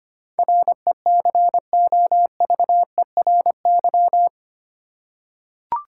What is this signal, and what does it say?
Morse code, heard as a keyed tone, about 25 wpm, 700 Hz. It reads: RECOVERY
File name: Eight Letter Words - 25wpm